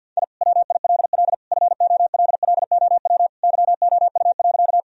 Morse code, heard as a keyed tone, about 51 wpm, 700 Hz. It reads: I WILL FOLLOW YOU.
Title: Sentences from Top 1000 Words - Review Part 1 - 50wpm